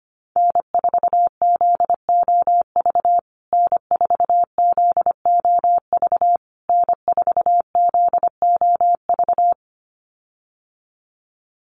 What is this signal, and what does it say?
Morse code, heard as a keyed tone, about 25 wpm, 700 Hz. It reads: N4ZOV N4ZOV N4ZOV
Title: USA Call Signs Repeated 3x 25wpm